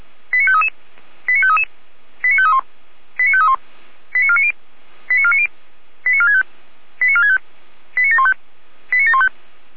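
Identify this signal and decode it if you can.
Five-tone selective call (ZVEI), heard as quick runs of five tones, tones of about 70 ms. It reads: 89533 89533 89531 89531 89490 89490 89465 89465 89715 89715
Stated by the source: ZVEI five-tone selective call received by SDR.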